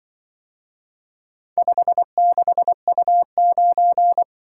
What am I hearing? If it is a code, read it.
Morse code, heard as a keyed tone, about 24 wpm, 700 Hz. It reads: 56U9
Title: Speed-Racing - Four Letter-Number Combinations - 20wpm